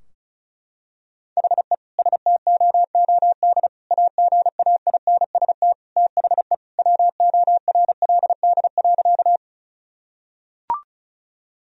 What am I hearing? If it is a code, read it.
Morse code, heard as a keyed tone, about 36 wpm, 700 Hz. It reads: HE STOOD AGAINST THE WORLD.
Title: Sentences from Top 1000 Words - Review Part 2 - 35wpm